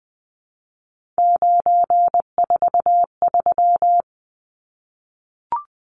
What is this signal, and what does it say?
Morse code, heard as a keyed tone, about 20 wpm, 700 Hz. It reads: 943
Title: Numbers - 3-Digit 20wpm